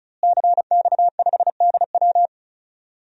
Morse code, heard as a keyed tone, about 34 wpm, 700 Hz. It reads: CX5DW
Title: DXCC-100 Call Signs - 5 Characters - Repeated 3x - 35wpm